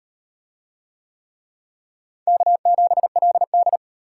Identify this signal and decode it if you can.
Morse code, heard as a keyed tone, about 38 wpm, 700 Hz. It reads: K7LD